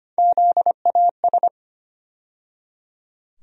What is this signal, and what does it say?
Morse code, heard as a keyed tone, about 25 wpm, 700 Hz. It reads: ZAS
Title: USA Call Sign Postfixes - 3 Letters 25wpm